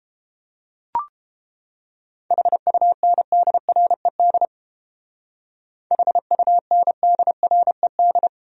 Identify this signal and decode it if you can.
Morse code, heard as a keyed tone, about 30 wpm, 700 Hz. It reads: HUNDRED HUNDRED